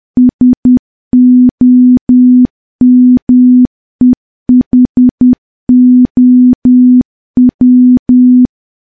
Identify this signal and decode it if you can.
Morse code, heard as a keyed tone, about 10 wpm, 250 Hz. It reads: SOMEHOW
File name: morse / morse-training-10WPM